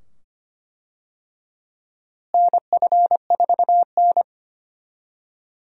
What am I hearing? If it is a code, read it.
Morse code, heard as a keyed tone, about 25 wpm, 700 Hz. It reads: NF4N
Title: USA Call Signs - 4 Characters - Repeated 2x - 25wpm